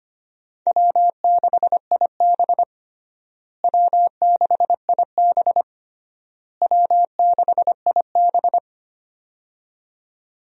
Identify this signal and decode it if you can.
Morse code, heard as a keyed tone, about 25 wpm, 700 Hz. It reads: W6IB W6IB W6IB
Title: USA Call Signs - 4 Characters - Repeated 3x - 25wpm